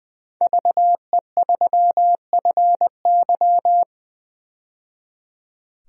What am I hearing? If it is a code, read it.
Morse code, heard as a keyed tone, about 20 wpm, 700 Hz. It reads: VE3FY
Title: DXCC-100 Call Signs - 5 Characters - 20wpm